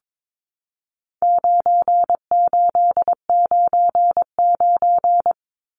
Morse code, heard as a keyed tone, about 22 wpm, 700 Hz. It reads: 9899
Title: Numbers - 4-Digit 22wpm